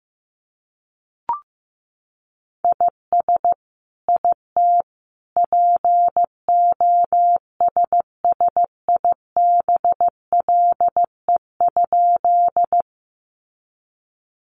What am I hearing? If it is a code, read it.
Morse code, heard as a keyed tone, about 15 wpm, 700 Hz. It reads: IS IT POSSIBLE?
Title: Sentences from Top 500 Words 15wpm